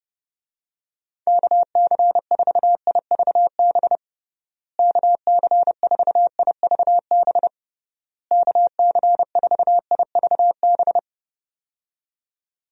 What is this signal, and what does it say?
Morse code, heard as a keyed tone, about 30 wpm, 700 Hz. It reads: KC4IVB KC4IVB KC4IVB
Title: USA Call Signs - 6 Characters - Repeated 3x - 30wpm